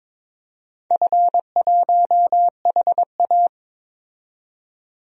Morse code, heard as a keyed tone, about 22 wpm, 700 Hz. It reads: F1HA
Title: DXCC-100 Call Signs - 4 Characters - 22wpm